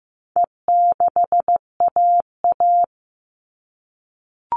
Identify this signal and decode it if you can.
Morse code, heard as a keyed tone, about 15 wpm, 700 Hz. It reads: E6AA
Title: All Cut Numbers - 4-Digit 15wpm